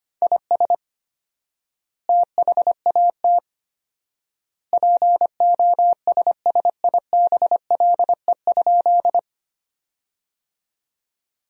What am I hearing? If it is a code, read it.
Morse code, heard as a keyed tone, about 25 wpm, 700 Hz. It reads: IS THAT POSSIBLE?
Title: 4x Word-Spacing - Sentences from Top 500 Words - 25wpm